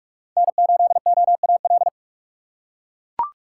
Morse code, heard as a keyed tone, about 42 wpm, 700 Hz. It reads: N8OAL